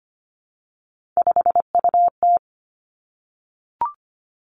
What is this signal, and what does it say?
Morse code, heard as a keyed tone, about 25 wpm, 700 Hz. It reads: 5UT
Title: Three Challenging Letter-Number Combinations 25wpm